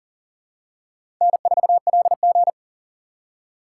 Morse code, heard as a keyed tone, about 40 wpm, 700 Hz. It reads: N4LG
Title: USA Call Signs - 4 Characters - 4x Character Spacing - 40wpm